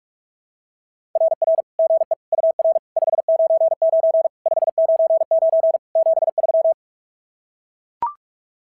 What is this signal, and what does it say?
Morse code, heard as a keyed tone, about 44 wpm, 690 Hz. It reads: RR GE UR 599 599 73
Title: SOTA QSO Practice 45wpm